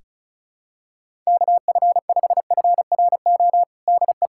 Morse code, heard as a keyed tone, about 36 wpm, 700 Hz. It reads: KF5FRO DE
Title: USA Callsign DE Callsign - 6 Characters - 35wpm